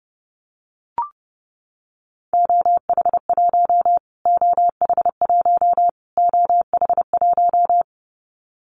Morse code, heard as a keyed tone, about 30 wpm, 700 Hz. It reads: OH1 OH1 OH1
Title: Callsign Prefixes for Top 25 DXCC - Repeated 3x - 30wpm